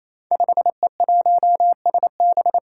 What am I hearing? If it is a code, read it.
Morse code, heard as a keyed tone, about 27 wpm, 700 Hz. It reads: 5E1SB
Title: Five Challenging Letter-Number Combinations 28wpm